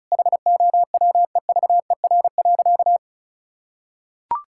Morse code, heard as a keyed tone, about 34 wpm, 700 Hz. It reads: HOWEVER.
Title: Sentences from Top 1000 Words - Review Part 6 - 35wpm